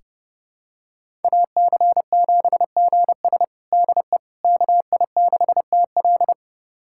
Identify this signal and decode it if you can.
Morse code, heard as a keyed tone, about 32 wpm, 700 Hz. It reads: AC7GS DE KI6TL
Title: USA Callsign DE Callsign - 5 Characters - 30wpm